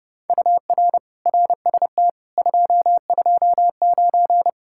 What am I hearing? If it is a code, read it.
Morse code, heard as a keyed tone, about 30 wpm, 700 Hz. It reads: UR RST 229